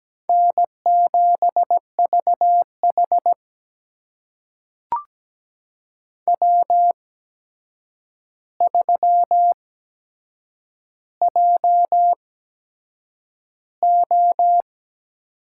Morse code, heard as a keyed tone, about 17 wpm, spaced as for 7 wpm, 700 Hz. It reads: N7VH W3JO
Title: USA Call Signs - 4 Characters - 8x Character Spacing - 17wpm